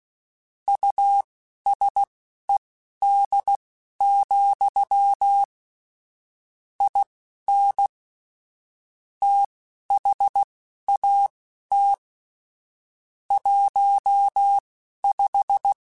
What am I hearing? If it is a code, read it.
Morse code, heard as a keyed tone, about 16 wpm, spaced as for 11 wpm, 800 Hz. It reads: USED, IN THAT 15
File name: Technical Subjects / Technical Subjects-mp3s / Sw Pwr Sup @10